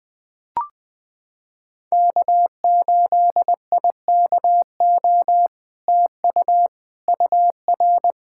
Morse code, heard as a keyed tone, about 20 wpm, 700 Hz. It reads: K8IKO TU UR
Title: POTA QSO Practice - 20wpm